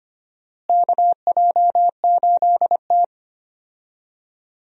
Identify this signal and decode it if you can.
Morse code, heard as a keyed tone, about 25 wpm, 700 Hz. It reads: KJ8T
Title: USA Call Signs - 4 Characters - Repeated 2x - 25wpm